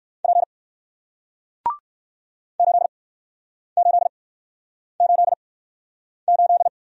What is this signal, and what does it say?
Morse code, heard as a keyed tone, about 60 wpm, spaced as for 9 wpm, 700 Hz. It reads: P 8888